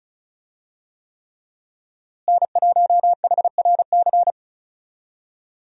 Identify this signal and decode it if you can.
Morse code, heard as a keyed tone, about 36 wpm, 700 Hz. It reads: N1HRC
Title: USA Call Signs - 5 Characters - Repeated 3x - 35wpm